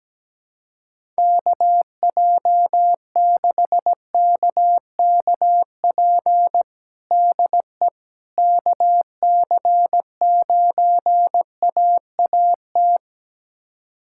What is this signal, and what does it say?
Morse code, heard as a keyed tone, about 17 wpm, 700 Hz. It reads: KJ6KKP DE KC9AAT
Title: USA Callsign DE Callsign - 6 Characters - 17wpm